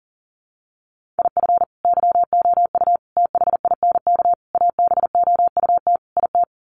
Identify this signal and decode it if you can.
Morse code, heard as a keyed tone, about 40 wpm, 700 Hz. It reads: IF YOU THINK ABOUT IT
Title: Sets of 5 Words Encore2 40wpm